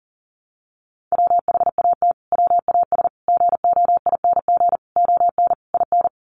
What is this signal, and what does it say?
Morse code, heard as a keyed tone, about 40 wpm, 700 Hz. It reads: WHAT WAS GOING ON IN
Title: Sets of 5 Words Encore 40wpm